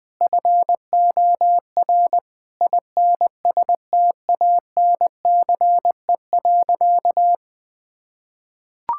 Morse code, heard as a keyed tone, about 21 wpm, 700 Hz. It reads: FOR INSTANCE.
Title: Sentences from Top 1000 Words - Review Part 5 - 20wpm